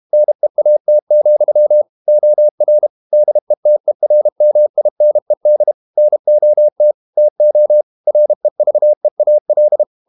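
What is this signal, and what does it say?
Morse code, heard as a keyed tone, about 32 wpm, 600 Hz. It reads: NEAT, OR DETERMINED NOT TO REVEAL
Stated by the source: Morse code practice at 32 wpm from CW med Gnister, reading Ken Follett's novel Never in English.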